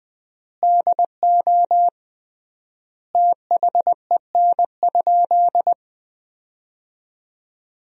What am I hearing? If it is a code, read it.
Morse code, heard as a keyed tone, about 20 wpm, 700 Hz. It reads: DO THEN?
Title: 3x Word-Spacing - Sentences from Top 100 Words - 20wpm